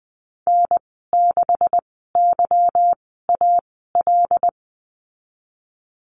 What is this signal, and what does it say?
Morse code, heard as a keyed tone, about 20 wpm, spaced as for 15 wpm, 700 Hz. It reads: N6YAL
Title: 167 Lesson 27g - Number 7 - Call Signs 20wpm